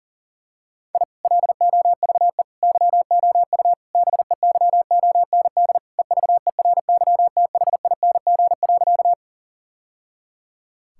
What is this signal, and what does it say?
Morse code, heard as a keyed tone, about 40 wpm, 700 Hz. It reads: I LOVE YOU BEYOND EVERYTHING.
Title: Sentences from Top 1000 Words - Review Part 6 - 40wpm